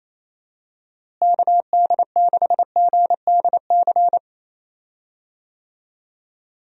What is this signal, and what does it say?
Morse code, heard as a keyed tone, about 28 wpm, 700 Hz. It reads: KD6GDC
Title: USA Call Signs - 6 Characters - Repeated 2x - 28wpm